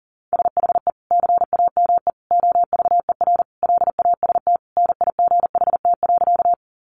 Morse code, heard as a keyed tone, about 40 wpm, 700 Hz. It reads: SHE CAME OVER LAST NIGHT.
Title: Sentences from Top 300 Words 40wpm